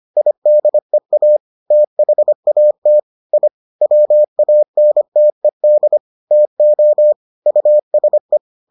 Morse code, heard as a keyed tone, about 25 wpm, 600 Hz. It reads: IDEA THAT I WANTED TO USE